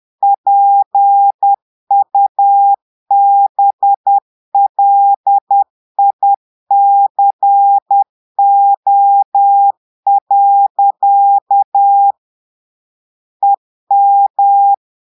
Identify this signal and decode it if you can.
Morse code, heard as a keyed tone, about 10 wpm, 800 Hz. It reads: PUBLICO. EM
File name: portugais texte en clair 2 vitesse 10 mots minutes